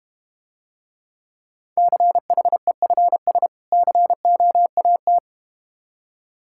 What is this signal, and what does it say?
Morse code, heard as a keyed tone, about 31 wpm, 700 Hz. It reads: CHEFS COAT